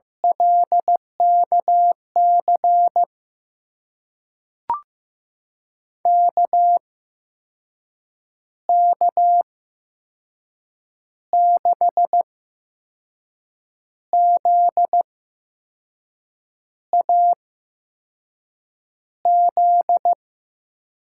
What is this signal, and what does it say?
Morse code, heard as a keyed tone, about 15 wpm, spaced as for 5 wpm, 700 Hz. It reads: LKC KK6ZAZ
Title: USA Call Signs - 6 Characters - 8x Character Spacing - 15wpm